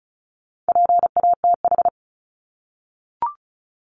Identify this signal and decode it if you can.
Morse code, heard as a keyed tone, about 34 wpm, 700 Hz. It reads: PATH
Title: Top 1251-1500 Words 35wpm